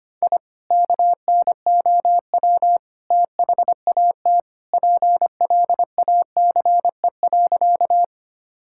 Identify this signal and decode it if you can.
Morse code, heard as a keyed tone, about 25 wpm, 700 Hz. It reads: I KNOW THAT PLACE.
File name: practice / 25wpm / Sentences with 2-Word Phrases - 25wpm